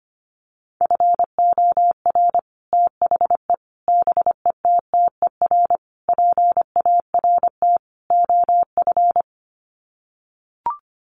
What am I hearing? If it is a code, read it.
Morse code, heard as a keyed tone, about 25 wpm, 700 Hz. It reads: FOR THE BETTER PART OF
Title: Sets of 5 Words Encore2 25wpm